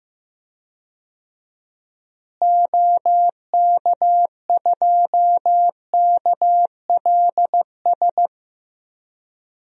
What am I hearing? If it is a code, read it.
Morse code, heard as a keyed tone, about 15 wpm, 700 Hz. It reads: OK2KLS
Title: DXCC-100 Call Signs - 6 Characters - Repeated 3x - 15wpm